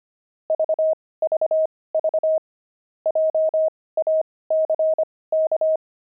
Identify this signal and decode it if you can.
Morse code, heard as a keyed tone, about 25 wpm, spaced as for 18 wpm, 630 Hz. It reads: VVV JACK
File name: pangram-fs-18-cs-25-antique